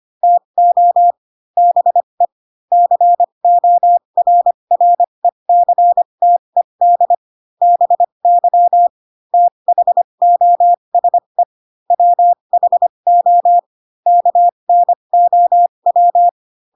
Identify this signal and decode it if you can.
Morse code, heard as a keyed tone, about 25 wpm, spaced as for 22 wpm, 700 Hz. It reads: TO BE CORRECTED BY THOSE WHO KNOW
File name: Pinocchio / Pinocchio ch4 18 25 ik1hge